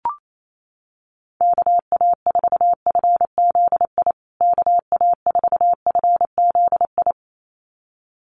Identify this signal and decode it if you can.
Morse code, heard as a keyed tone, about 28 wpm, 700 Hz. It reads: KA4FZI KA4FZI
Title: USA Call Signs Repeated 2x 28wpm